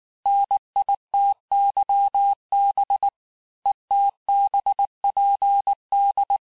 Morse code, heard as a keyed tone, about 19 wpm, 800 Hz. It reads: NITYB ETBPD
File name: cours-du-mercredi / audios / cours-869-texte-CHIFFRE-vitesse-18-MPM